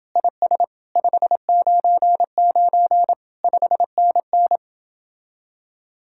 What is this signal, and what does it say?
Morse code, heard as a keyed tone, about 29 wpm, 700 Hz. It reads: IS 599 5NN